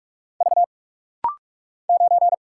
Morse code, heard as a keyed tone, about 48 wpm, 700 Hz. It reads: V 9
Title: Single Letter-Number - Rapid-Fire 45wpm